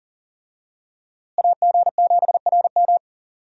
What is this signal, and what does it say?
Morse code, heard as a keyed tone, about 40 wpm, 700 Hz. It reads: AG7RM
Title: USA Call Signs - 5 Characters - Repeated 3x - 40wpm